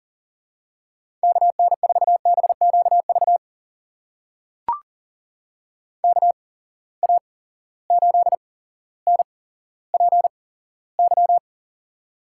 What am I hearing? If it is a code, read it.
Morse code, heard as a keyed tone, about 40 wpm, spaced as for 17 wpm, 700 Hz. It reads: KN4BQV KA8NPY
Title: USA Call Signs - 6 Characters - 8x Character Spacing - 40wpm